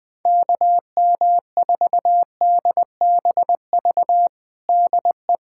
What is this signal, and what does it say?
Morse code, heard as a keyed tone, about 20 wpm, 700 Hz. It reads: KM4DBV DE
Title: USA Callsign DE Callsign - 6 Characters - 20wpm